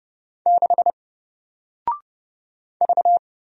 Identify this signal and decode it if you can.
Morse code, heard as a keyed tone, about 30 wpm, 700 Hz. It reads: 6 V
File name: Single-Letter-Number-30wpm